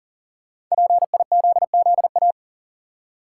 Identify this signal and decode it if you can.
Morse code, heard as a keyed tone, about 40 wpm, 700 Hz. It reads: PIZZA
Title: Five Letter Words - 40wpm